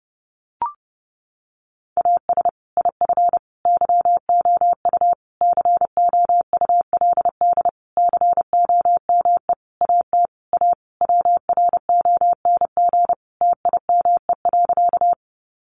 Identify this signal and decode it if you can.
Morse code, heard as a keyed tone, about 28 wpm, 700 Hz. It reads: AS IF YOU COULD COME AT A WRONG TIME.
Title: Sentences from Top 900 Words 30wpm